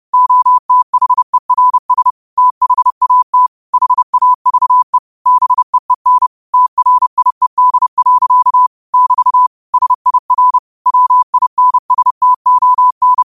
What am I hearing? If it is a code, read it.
Morse code, heard as a keyed tone, about 30 wpm, 1000 Hz. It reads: OTHERS THAT HAVE BEEN TRIED. <BT> SIR WINSTON
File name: qotdfiles / 30wpm